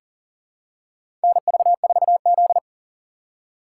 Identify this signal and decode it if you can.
Morse code, heard as a keyed tone, about 40 wpm, 700 Hz. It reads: NV4Z
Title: USA Call Signs - 4 Characters - 4x Character Spacing - 40wpm